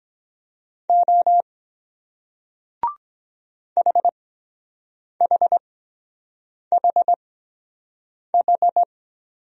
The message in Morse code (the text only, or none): O HHHH